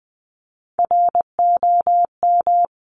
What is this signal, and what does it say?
Morse code, heard as a keyed tone, about 20 wpm, 700 Hz. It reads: ROM
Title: Top Three Letter Combinations in Words 20wpm